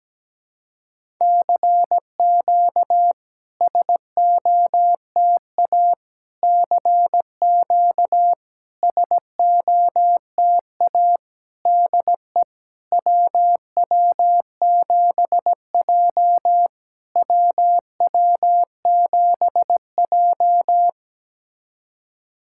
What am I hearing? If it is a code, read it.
Morse code, heard as a keyed tone, about 17 wpm, 660 Hz. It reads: CQ SOTA CQ SOTA DE WW7J WW7J
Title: SOTA QSO Practice 17wpm